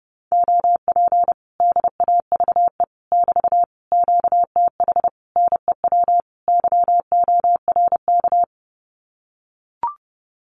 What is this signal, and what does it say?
Morse code, heard as a keyed tone, about 30 wpm, 700 Hz. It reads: OP DAVE <BT> QTH NEW YORK
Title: Common QSO Phrases Encore 30wpm